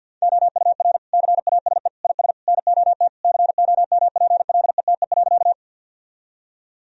Morse code, heard as a keyed tone, about 50 wpm, 700 Hz. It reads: OUR CASE IS NOT COMPLETE.
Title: Sentences from Top 1000 Words - Review Part 2 - 50wpm